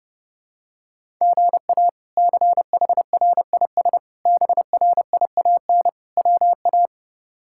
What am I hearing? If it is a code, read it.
Morse code, heard as a keyed tone, about 32 wpm, 700 Hz. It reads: GA CHRIS BRIAN WA
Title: K1USN SST QSO Practice - 30wpm